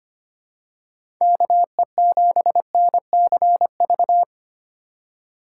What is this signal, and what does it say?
Morse code, heard as a keyed tone, about 29 wpm, 700 Hz. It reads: KE7NCV
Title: Speed Racing - USA 2x3 Call Signs 25wpm